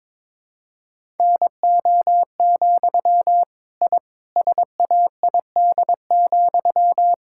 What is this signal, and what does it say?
Morse code, heard as a keyed tone, about 22 wpm, 700 Hz. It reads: NO, I SAID,
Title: Sentences from Top 1000 Words - Review Part 6 - 22wpm